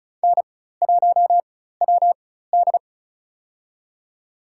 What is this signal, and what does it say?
Morse code, heard as a keyed tone, about 34 wpm, spaced as for 16 wpm, 700 Hz. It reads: N1WD
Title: USA Call Signs - 4 Characters - 4x Character Spacing - 35wpm